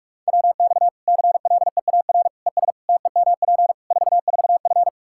Morse code, heard as a keyed tone, about 41 wpm, 700 Hz. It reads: WX CLEAR ES TEMP 44F
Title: Practice QSO - Exchange 2 - Info - 45wpm